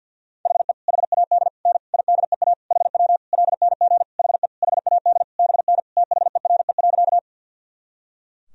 Morse code, heard as a keyed tone, about 50 wpm, 700 Hz. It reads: HE HAD N IDEA HW LNG HE HAD BN THERE.
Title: Sentences from Top 300 Words and Abbreviations 50wpm